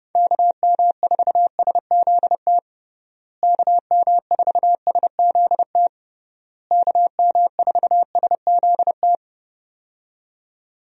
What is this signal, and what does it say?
Morse code, heard as a keyed tone, about 30 wpm, 700 Hz. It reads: KM4SZT KM4SZT KM4SZT
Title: USA Call Signs - 6 Characters - Repeated 3x - 30wpm